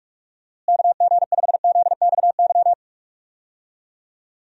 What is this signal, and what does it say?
Morse code, heard as a keyed tone, about 41 wpm, 700 Hz. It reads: KG5ZXY
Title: USA Call Signs - 6 Characters - Repeated 2x - 45wpm